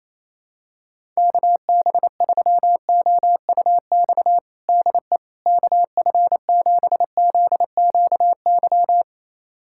Morse code, heard as a keyed tone, about 28 wpm, 700 Hz. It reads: KB3OUX DE KF7ZQY
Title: USA Callsign DE Callsign - 6 Characters - 28wpm